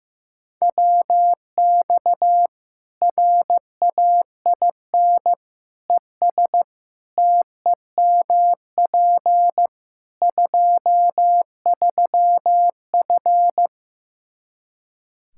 Morse code, heard as a keyed tone, about 16 wpm, 700 Hz. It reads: WX RAIN ES TEMP 23F
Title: Practice QSO - Exchange 2 - Info - 15wpm